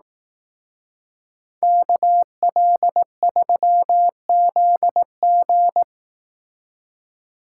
Morse code, heard as a keyed tone, about 18 wpm, 700 Hz. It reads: KL3ZG
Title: Speed Racing - USA 2x2 Call Signs 15wpm